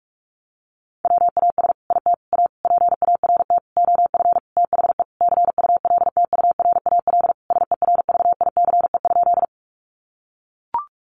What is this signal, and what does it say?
Morse code, heard as a keyed tone, about 44 wpm, 700 Hz. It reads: WAS IT A PART OF THE CULTURAL SERVICE?
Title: Sentences from Top 1500 Words 45wpm